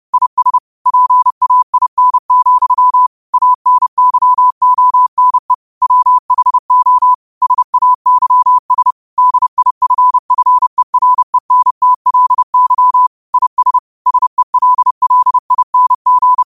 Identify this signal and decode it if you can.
Morse code, heard as a keyed tone, about 30 wpm, 1000 Hz. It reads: IS PAIN, ANYONE WHO SAYS DIFFERENTLY IS SELLING
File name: 30wpm